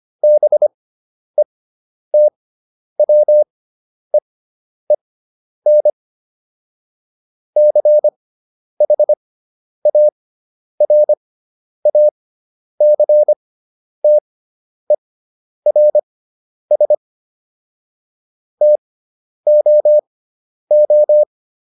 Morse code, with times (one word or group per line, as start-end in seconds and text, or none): 0.2-5.9 BETWEEN
7.6-16.9 CHARACTERS
18.6-21.2 TOO